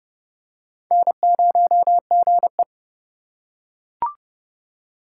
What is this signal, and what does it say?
Morse code, heard as a keyed tone, about 30 wpm, 700 Hz. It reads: N0GE
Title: USA Call Signs - 4 Characters - 8x Character Spacing - 30wpm